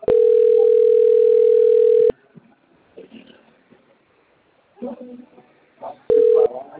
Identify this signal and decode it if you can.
Ringback tone (North America), 440 + 480 Hz, with speech.
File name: out- - - - - .36154